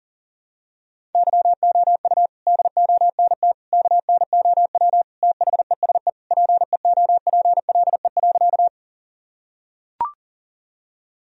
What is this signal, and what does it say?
Morse code, heard as a keyed tone, about 40 wpm, 700 Hz. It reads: YOU DONT KNOW THESE PEOPLE.
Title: Sentences from Top 1000 Words - Review Part 1 - 40wpm